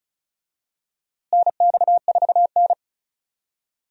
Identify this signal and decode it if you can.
Morse code, heard as a keyed tone, about 36 wpm, 700 Hz. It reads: NX4N
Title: USA Call Signs - 4 Characters - Repeated 3x - 35wpm